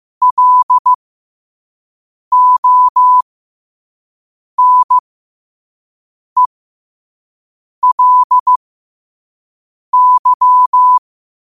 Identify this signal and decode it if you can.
Morse code, heard as a keyed tone, about 15 wpm, spaced as for 5 wpm, 1000 Hz. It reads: LONELY
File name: qotdfiles / 5wpm